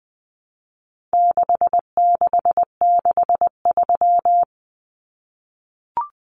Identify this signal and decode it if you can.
Morse code, heard as a keyed tone, about 20 wpm, 700 Hz. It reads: 6663
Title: Numbers - 4-Digit 20wpm